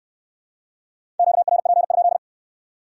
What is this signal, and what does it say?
Morse code, heard as a keyed tone, about 66 wpm, 700 Hz. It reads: QRP?